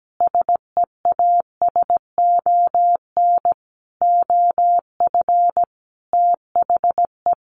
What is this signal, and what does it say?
Morse code, heard as a keyed tone, about 17 wpm, 700 Hz. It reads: SEASON OF THE
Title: Sentences from Top 1400 Words 17wpm